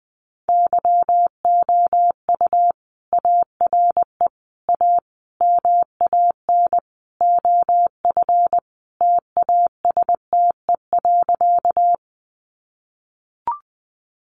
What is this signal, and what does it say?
Morse code, heard as a keyed tone, about 20 wpm, 700 Hz. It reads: YOU ARE A MAN OF TASTE.
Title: Sentences from Top 1100 Words 20wpm